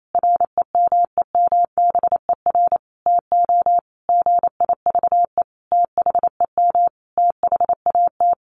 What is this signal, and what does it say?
Morse code, heard as a keyed tone, about 28 wpm, 700 Hz. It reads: REMEMBER TO GIVE THEM THAT
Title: Sentences from Top 1100 Words 28wpm